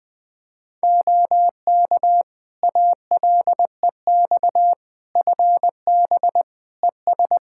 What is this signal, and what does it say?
Morse code, heard as a keyed tone, about 20 wpm, 700 Hz. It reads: OK ALEX FB ES
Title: Practice QSO - Exchange 2 - Info - 20wpm